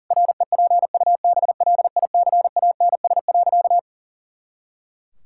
Morse code, heard as a keyed tone, about 40 wpm, 700 Hz. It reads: REPUBLICANS.
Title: Pangram Challenge v2 - 40wpm